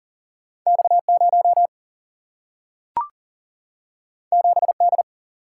Morse code, heard as a keyed tone, about 39 wpm, 700 Hz. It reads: X0 7D
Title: Single-Letter Single-Number 40wpm